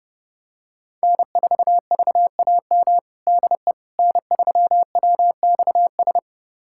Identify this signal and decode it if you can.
Morse code, heard as a keyed tone, about 30 wpm, 700 Hz. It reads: N4VAM DE N3WXS
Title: USA Callsign DE Callsign - 5 Characters - 30wpm